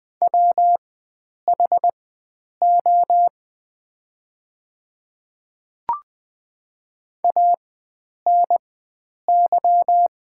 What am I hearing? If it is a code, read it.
Morse code, heard as a keyed tone, about 20 wpm, spaced as for 9 wpm, 690 Hz. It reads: WHO ANY
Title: 4x Character Spacing - Top 300 Words - 20wpm